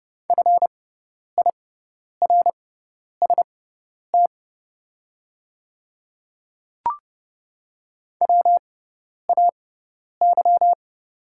Morse code, heard as a keyed tone, about 30 wpm, spaced as for 10 wpm, 700 Hz. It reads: FIRST WAY
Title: 6x Character Spacing - Top 200 Words - 30wpm